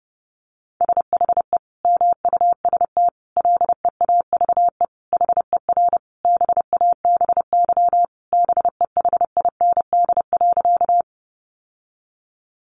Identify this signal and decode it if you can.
Morse code, heard as a keyed tone, about 30 wpm, 700 Hz. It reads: SHE MUST LEAVE HER BABY BEHIND.
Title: Sentences from Top 1300 Words 30wpm